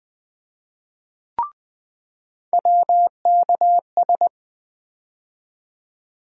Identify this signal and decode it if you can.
Morse code, heard as a keyed tone, about 20 wpm, 700 Hz. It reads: WKS
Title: USA Call Sign Postfixes - 3 Letters 20wpm